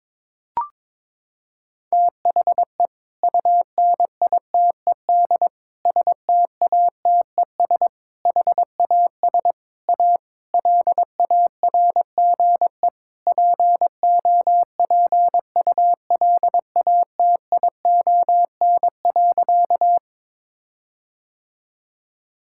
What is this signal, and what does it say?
Morse code, heard as a keyed tone, about 22 wpm, 700 Hz. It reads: THE UNITED STATES HAS A LARGE POPULATION.
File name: practice / 22wpm / Sentences with 4-Word Phrases - 22wpm